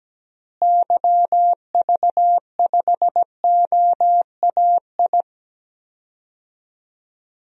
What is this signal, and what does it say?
Morse code, heard as a keyed tone, about 17 wpm, 700 Hz. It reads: YV5OAI
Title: DXCC-100 Call Signs - 6 Characters - Repeated 3x - 17wpm